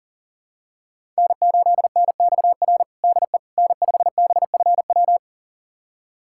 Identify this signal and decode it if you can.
Morse code, heard as a keyed tone, about 40 wpm, 700 Hz. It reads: N8NXR DE N5BFW
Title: USA Callsign DE Callsign - 5 Characters - 40wpm